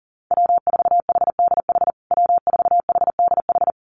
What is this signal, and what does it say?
Morse code, heard as a keyed tone, about 40 wpm, 700 Hz. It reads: W4HDH W4HDH
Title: USA Call Signs Repeated 2x 40wpm